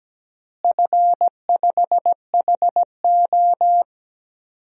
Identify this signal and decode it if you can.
Morse code, heard as a keyed tone, about 17 wpm, 700 Hz. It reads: F5HO
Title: DXCC-100 Call Signs - 4 Characters - Repeated 3x - 17wpm